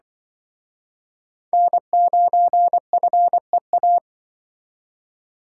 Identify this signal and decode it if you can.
Morse code, heard as a keyed tone, about 24 wpm, 700 Hz. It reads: N9FEA
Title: Speed Racing - USA 1x3 Call Signs 20wpm